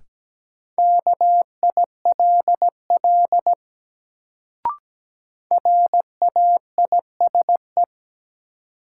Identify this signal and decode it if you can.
Morse code, heard as a keyed tone, about 17 wpm, 700 Hz. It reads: KILL RAISE
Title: Sentences from Top 1400 Words - New Word Drill - 17wpm